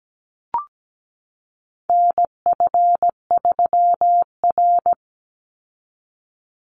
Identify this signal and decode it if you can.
Morse code, heard as a keyed tone, about 17 wpm, 700 Hz. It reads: NF3R
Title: Call Signs - 17wpm